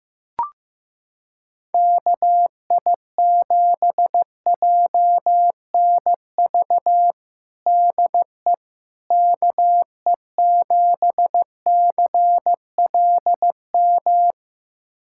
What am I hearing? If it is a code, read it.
Morse code, heard as a keyed tone, about 15 wpm, 700 Hz. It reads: KI7JNV DE KE7CLM
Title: USA Callsign DE Callsign - 6 Characters - 15wpm